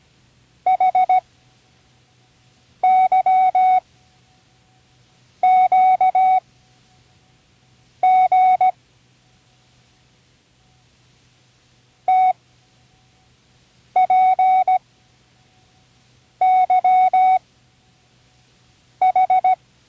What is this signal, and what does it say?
Morse code, heard as a keyed tone, about 17 wpm, spaced as for 5 wpm, 750 Hz. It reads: HYQG TPYH
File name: CWResources / Learning Morse Code (from Tune in the World) / 24 random